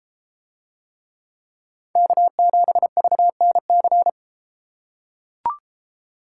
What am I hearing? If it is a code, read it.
Morse code, heard as a keyed tone, about 31 wpm, 700 Hz. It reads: K7VNC